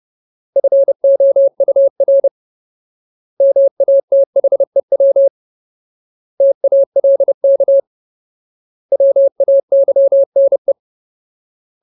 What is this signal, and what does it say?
Morse code, heard as a keyed tone, about 30 wpm, 550 Hz. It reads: FOUR MATHEW TALK WAYNE
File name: Ultimate Word List / 30WPM Ultimate